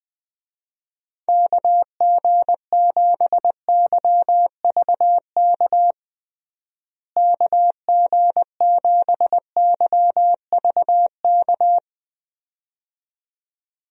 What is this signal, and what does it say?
Morse code, heard as a keyed tone, about 20 wpm, 700 Hz. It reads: KG7YVK KG7YVK
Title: USA Call Signs - 6 Characters - Repeated 2x - 20wpm